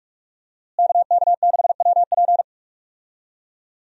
Morse code, heard as a keyed tone, about 42 wpm, 700 Hz. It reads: KK6WP
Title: USA Call Signs - 5 Characters - Repeated 3x - 45wpm